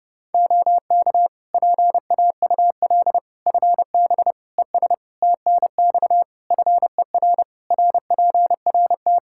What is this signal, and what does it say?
Morse code, heard as a keyed tone, about 30 wpm, 700 Hz. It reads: OK PAUL FB ES TNX FER RPRT
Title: K1USN SST QSO Practice 30wpm